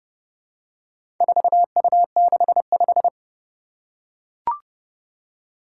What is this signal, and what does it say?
Morse code, heard as a keyed tone, about 30 wpm, 700 Hz. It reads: 4U65
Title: Four Challenging Letter-Number Combinations 30wpm